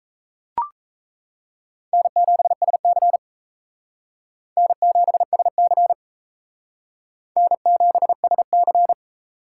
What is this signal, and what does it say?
Morse code, heard as a keyed tone, about 36 wpm, 700 Hz. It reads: N7SC N7SC N7SC